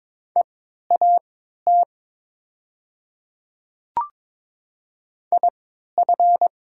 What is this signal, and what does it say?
Morse code, heard as a keyed tone, about 22 wpm, spaced as for 13 wpm, 700 Hz. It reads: EAT IF
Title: 3x Character Spacing - Top 300 Words - 22wpm